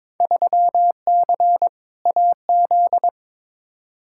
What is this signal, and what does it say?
Morse code, heard as a keyed tone, about 22 wpm, 700 Hz. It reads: 3C AZ
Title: Field Day Exchanges Encore - 22wpm